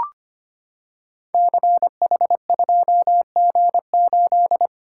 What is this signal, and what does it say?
Morse code, heard as a keyed tone, about 25 wpm, 690 Hz. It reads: CH2G8
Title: Code Groups - 5 Letter:Numbers - 25wpm